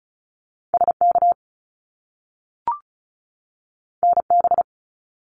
Morse code, heard as a keyed tone, about 36 wpm, 700 Hz. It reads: SK NB
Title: Canadian Provinces and Territories 35wpm